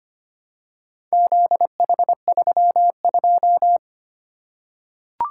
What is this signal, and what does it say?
Morse code, heard as a keyed tone, about 25 wpm, 700 Hz. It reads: ZH32
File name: practice / 25wpm / Four Letter-Number Combinations - 25wpm